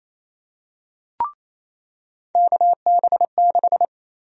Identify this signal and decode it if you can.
Morse code, heard as a keyed tone, about 28 wpm, 700 Hz. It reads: KB6
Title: USA Call Sign Prefixes - 28wpm